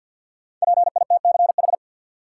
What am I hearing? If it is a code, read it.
Morse code, heard as a keyed tone, about 49 wpm, 700 Hz. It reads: PITCH
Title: Sentences from Top 1300 Words - New Word Drill - 50wpm